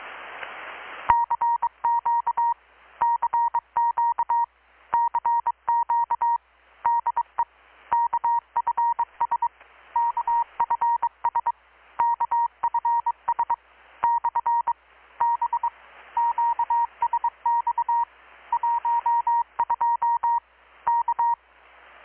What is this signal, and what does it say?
Morse code, heard as a keyed tone, about 22 wpm, 940 Hz. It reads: CQ CQ CQ DE KFS KFS KFS / B QSX 12 K
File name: blog / swl / 6-abril-2019-12695-USB-KPS